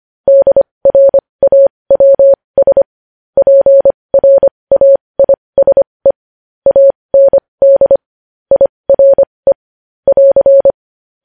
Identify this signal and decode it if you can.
Morse code, heard as a keyed tone, about 25 wpm, spaced as for 20 wpm, 550 Hz. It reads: DRAWS PRAISE AND IRE <AR>
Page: news-headlines-in-morse-code-at-20-wpm